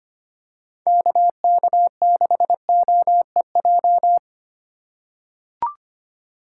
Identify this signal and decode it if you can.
Morse code, heard as a keyed tone, about 25 wpm, 700 Hz. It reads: KK6OEJ
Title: USA Call Signs - 6 Characters - 2x Character Spacing - 25wpm